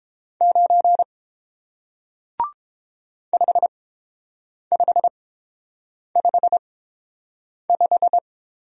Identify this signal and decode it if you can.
Morse code, heard as a keyed tone, about 30 wpm, spaced as for 7 wpm, 700 Hz. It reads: 9 5555